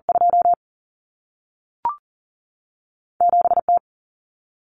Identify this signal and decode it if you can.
Morse code, heard as a keyed tone, about 40 wpm, 700 Hz. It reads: 2 7 T
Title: Common Cut Numbers - 2-Digit 40wpm